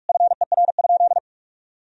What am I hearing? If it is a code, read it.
Morse code, heard as a keyed tone, about 47 wpm, 700 Hz. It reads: FER?
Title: Sentences from Top 200 Words and Abbreviations 45wpm